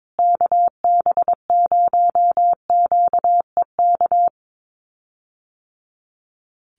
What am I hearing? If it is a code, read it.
Morse code, heard as a keyed tone, about 22 wpm, 700 Hz. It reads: KB0QEK
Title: USA Call Signs Repeated 2x 22wpm